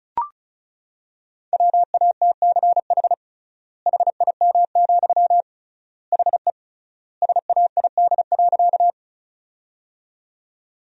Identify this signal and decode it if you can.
Morse code, heard as a keyed tone, about 36 wpm, 700 Hz. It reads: WATCH HIM, HE SAID.
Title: 3x Word-Spacing - Sentences from Top 300 Words - 35wpm